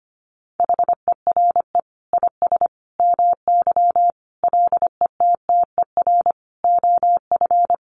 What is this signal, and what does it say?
Morse code, heard as a keyed tone, about 25 wpm, 700 Hz. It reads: HERE IS MY LETTER OF
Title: Sentences from Top 1400 Words 25wpm